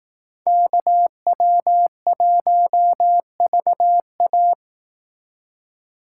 Morse code, heard as a keyed tone, about 16 wpm, 700 Hz. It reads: KW1VA
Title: Speed Racing - USA 2x2 Call Signs 15wpm